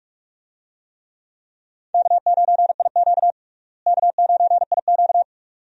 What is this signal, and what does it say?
Morse code, heard as a keyed tone, about 42 wpm, 700 Hz. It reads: K9IQ K9IQ
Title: USA Call Signs - 4 Characters - Repeated 2x - 45wpm